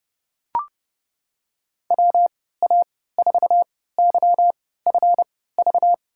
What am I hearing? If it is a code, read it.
Morse code, heard as a keyed tone, about 30 wpm, spaced as for 17 wpm, 700 Hz. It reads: WA4YFV
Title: USA Call Signs - 6 Characters - 3x Character Spacing - 30wpm